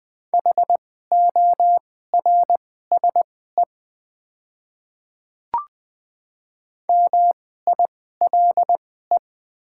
Morse code, heard as a keyed tone, about 20 wpm, spaced as for 14 wpm, 700 Hz. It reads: HORSE MILE
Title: 2x Character Spacing - Top 300 Words - 20wpm